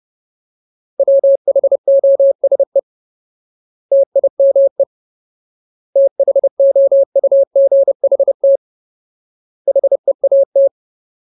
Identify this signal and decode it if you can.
Morse code, heard as a keyed tone, about 30 wpm, 550 Hz. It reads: WHOSE TIME THOUGHT HEAT
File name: Ultimate Word List / 30WPM Ultimate